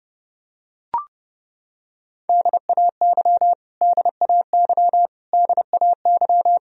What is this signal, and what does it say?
Morse code, heard as a keyed tone, about 30 wpm, 700 Hz. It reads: DAY DAY DAY
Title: Top 100 Words Repeated 30wpm